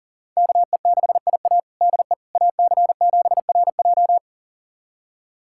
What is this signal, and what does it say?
Morse code, heard as a keyed tone, about 40 wpm, 700 Hz. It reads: KE6IA DE AC7RJ
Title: USA Callsign DE Callsign - 5 Characters - 40wpm